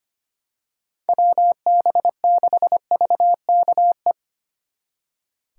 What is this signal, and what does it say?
Morse code, heard as a keyed tone, about 28 wpm, 700 Hz. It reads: WB6VKE
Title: Speed Racing - USA 2x3 Call Signs 25wpm